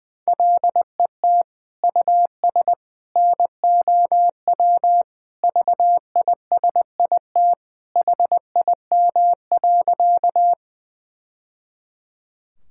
Morse code, heard as a keyed tone, about 20 wpm, 700 Hz. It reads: LET US NOW VISIT HIM.
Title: Sentences from Top 1000 Words - Review Part 6 - 20wpm